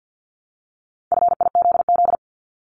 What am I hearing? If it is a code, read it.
Morse code, heard as a keyed tone, about 50 wpm, 690 Hz. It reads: FIZZ
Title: Multi-letter Phonemes 50wpm